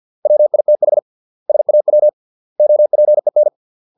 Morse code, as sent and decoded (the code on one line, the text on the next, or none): .-- .. - .... / ... .- .-- / --- .--. -.
WITH SAW OPEN